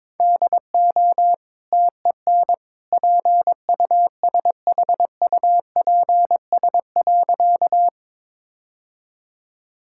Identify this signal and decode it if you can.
Morse code, heard as a keyed tone, about 21 wpm, 690 Hz. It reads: DO TEN PUSHUPS.
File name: practice / 22wpm / Easy Sentences with Hints - 22wpm